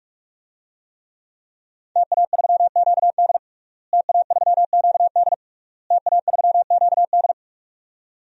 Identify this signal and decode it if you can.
Morse code, heard as a keyed tone, about 42 wpm, 700 Hz. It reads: TA3QD TA3QD TA3QD
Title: DXCC-100 Call Signs - 5 Characters - Repeated 3x - 45wpm